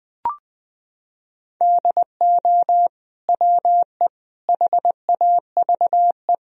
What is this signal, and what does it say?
Morse code, heard as a keyed tone, about 20 wpm, 700 Hz. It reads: DO WE HAVE
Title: Sentences from Top 1000 Words - Review Part 3 - 20wpm